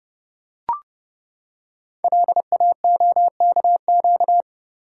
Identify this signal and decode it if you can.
Morse code, heard as a keyed tone, about 30 wpm, 700 Hz. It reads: LAOKQ
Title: Code Groups - 5 Letters - 30wpm